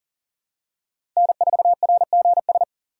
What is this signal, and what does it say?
Morse code, heard as a keyed tone, about 40 wpm, 700 Hz. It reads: N4RGS
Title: USA Call Signs - 5 Characters - 4x Character Spacing - 40wpm